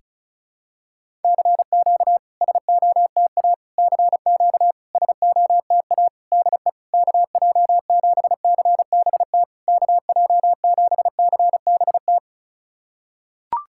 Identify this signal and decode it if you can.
Morse code, heard as a keyed tone, about 35 wpm, 690 Hz. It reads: CQ SOTA CQ SOTA DE KJ7CBT KJ7CBT
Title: SOTA QSO Practice 35wpm